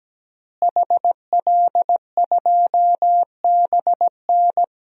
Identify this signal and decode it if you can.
Morse code, heard as a keyed tone, about 17 wpm, 700 Hz. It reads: HL2BN
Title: DXCC-100 Call Signs - 5 Characters - Repeated 2x - 17wpm